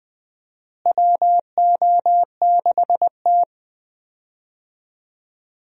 Morse code, heard as a keyed tone, about 20 wpm, 700 Hz. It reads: WO6T
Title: USA Call Signs - 4 Characters - Repeated 3x - 20wpm